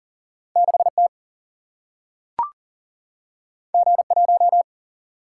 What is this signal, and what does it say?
Morse code, heard as a keyed tone, about 38 wpm, 700 Hz. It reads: 6 T G 1
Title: Single-Letter Single-Number 40wpm